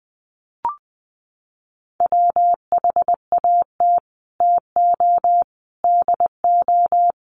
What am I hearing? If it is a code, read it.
Morse code, heard as a keyed tone, about 20 wpm, 700 Hz. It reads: WHAT TO DO
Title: Sets of 3 Words 20wpm